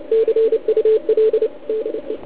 Morse code, heard as a keyed tone, about 29 wpm, 410 Hz, with radio noise. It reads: CUL DE